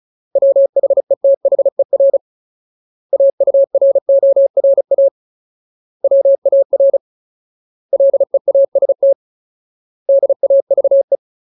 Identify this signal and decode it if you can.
Morse code, heard as a keyed tone, about 34 wpm, 550 Hz. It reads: WHETHER AURORA WAR LEAST DAVE
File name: Ultimate Word List / 35WPM Ultimate